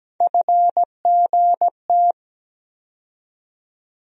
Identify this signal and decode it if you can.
Morse code, heard as a keyed tone, about 17 wpm, 700 Hz. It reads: FGT
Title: USA Call Sign Postfixes - 3 Letters 17wpm